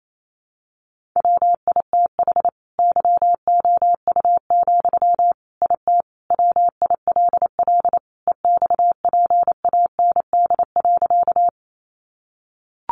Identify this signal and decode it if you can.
Morse code, heard as a keyed tone, about 28 wpm, 700 Hz. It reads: WITH YOU, IT WILL EXPAND.
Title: Sentences from Top 1400 Words 28wpm